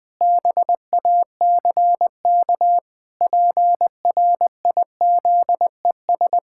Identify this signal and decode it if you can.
Morse code, heard as a keyed tone, about 20 wpm, 700 Hz. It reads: BACK PRIZES
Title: Pangram Challenge v2 - 20wpm